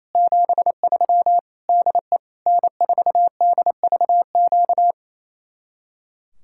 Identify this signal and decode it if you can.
Morse code, heard as a keyed tone, about 34 wpm, 700 Hz. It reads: 73 DE N4DVQ
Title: Speed-Racing Common QSO Phrases - 28wpm